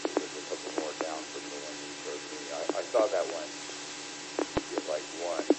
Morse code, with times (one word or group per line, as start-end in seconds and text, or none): none